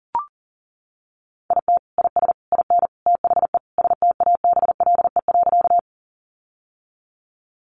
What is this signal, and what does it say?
Morse code, heard as a keyed tone, about 40 wpm, 700 Hz. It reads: IT IS IN THE STABLE.
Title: Sentences from Top 1500 Words 40wpm